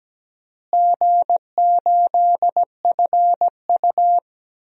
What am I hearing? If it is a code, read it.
Morse code, heard as a keyed tone, about 17 wpm, 700 Hz. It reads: G8FU
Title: DXCC-100 Call Signs - 4 Characters - Repeated 2x - 17wpm